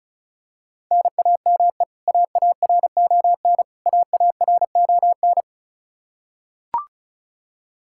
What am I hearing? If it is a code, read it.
Morse code, heard as a keyed tone, about 34 wpm, 700 Hz. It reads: NAME AARON AARON
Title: Practice QSO - Exchange 1 - Beginning - 35wpm